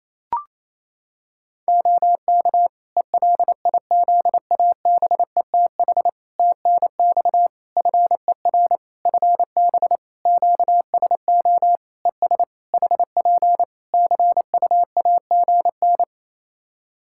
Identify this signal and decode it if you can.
Morse code, heard as a keyed tone, about 28 wpm, 700 Hz. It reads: OK ELIZABETH TNX FER FB QSO ES HP CUAGN
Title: Practice QSO - Exchange 4 - Ending - 28wpm